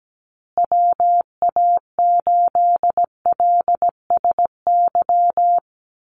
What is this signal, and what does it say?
Morse code, heard as a keyed tone, about 17 wpm, 700 Hz. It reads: WA8LSY
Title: USA Call Signs Repeated 3x 17wpm